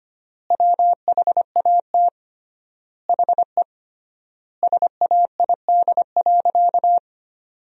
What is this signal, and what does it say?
Morse code, heard as a keyed tone, about 25 wpm, 700 Hz. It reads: WHAT HE SAID.
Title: 3x Word-Spacing - Sentences from Top 400 Words - 25wpm